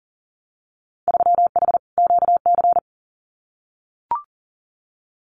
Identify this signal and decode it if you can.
Morse code, heard as a keyed tone, about 40 wpm, 700 Hz. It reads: 3H QC
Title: Winter Field Day Exchanges 40wpm